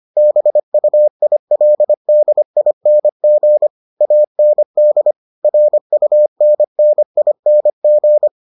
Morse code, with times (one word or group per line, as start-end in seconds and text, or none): 0.2-3.7 BUILDING
4.0-5.1 AND
5.4-8.3 RUNNING